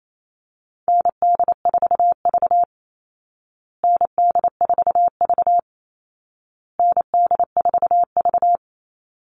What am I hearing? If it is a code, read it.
Morse code, heard as a keyed tone, about 28 wpm, 700 Hz. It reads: ND4V ND4V ND4V
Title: Call Signs Repeated 2x - 28wpm